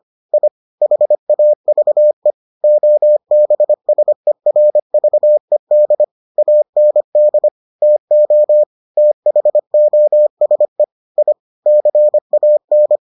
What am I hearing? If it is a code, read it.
Morse code, heard as a keyed tone, about 25 wpm, 600 Hz. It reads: I HAVE OBSERVED AND TO THOSE I CAN